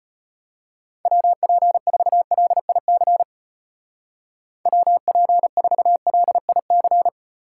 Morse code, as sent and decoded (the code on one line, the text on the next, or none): .-- .--. ....- .-.. .. -.-. / .-- .--. ....- .-.. .. -.-.
WP4LIC WP4LIC